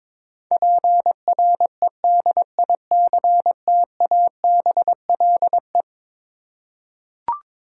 Morse code, heard as a keyed tone, about 21 wpm, 700 Hz. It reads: PREDICTABLE
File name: practice / 22wpm / Eleven Letter Words - 22wpm